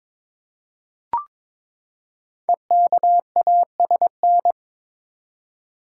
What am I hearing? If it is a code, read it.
Morse code, heard as a keyed tone, about 22 wpm, 700 Hz. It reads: EKASN
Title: Code Groups - 5 Letters - 22wpm